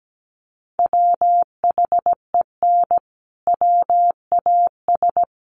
Morse code, heard as a keyed tone, about 17 wpm, 700 Hz. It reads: WHEN WAS
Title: Sentences from Top 100 Words 17wpm